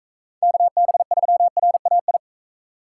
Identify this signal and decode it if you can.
Morse code, heard as a keyed tone, about 42 wpm, 700 Hz. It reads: KB3RAI